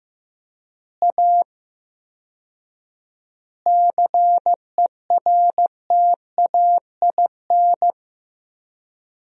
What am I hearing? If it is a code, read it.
Morse code, heard as a keyed tone, about 15 wpm, 700 Hz. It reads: A CERTAIN
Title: 4x Word-Spacing - Sentences from Top 500 Words - 15wpm